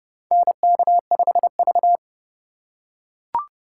Morse code, heard as a keyed tone, about 30 wpm, 700 Hz. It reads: NK5V
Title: USA Call Signs - 3x Character Spacing - 30wpm